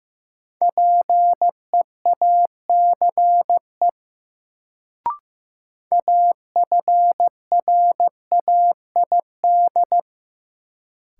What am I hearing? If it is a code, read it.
Morse code, heard as a keyed tone, about 15 wpm, 700 Hz. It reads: PEACE AFRAID
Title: Sentences from Top 1500 Words - New Word Drill - 15wpm